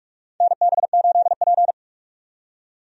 Morse code, heard as a keyed tone, about 46 wpm, 700 Hz. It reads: NB8P
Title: USA Call Signs - 4 Characters - 45wpm